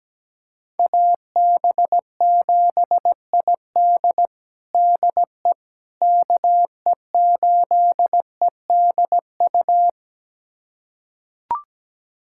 Morse code, heard as a keyed tone, about 17 wpm, 700 Hz. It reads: AB7ID DE KE8EDU
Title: USA Callsign DE Callsign 17wpm